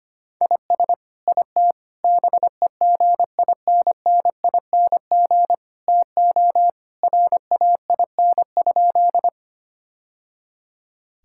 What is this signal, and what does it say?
Morse code, heard as a keyed tone, about 25 wpm, 700 Hz. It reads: IS IT BEGINNING TO RAIN?
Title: Sentences from Top 1000 Words - Review Part 5 - 25wpm